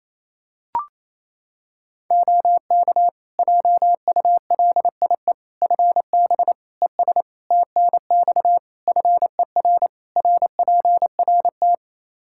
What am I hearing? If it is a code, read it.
Morse code, heard as a keyed tone, about 28 wpm, 700 Hz. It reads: OK JULIE FB ES TNX FER RPRT
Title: Practice QSO - Exchange 2 - Info - 28wpm